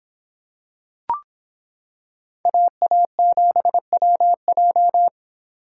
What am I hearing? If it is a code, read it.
Morse code, heard as a keyed tone, about 25 wpm, 700 Hz. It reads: AA7WJ